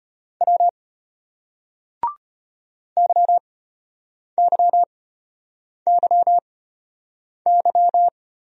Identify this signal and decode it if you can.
Morse code, heard as a keyed tone, about 33 wpm, spaced as for 8 wpm, 700 Hz. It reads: W YYYY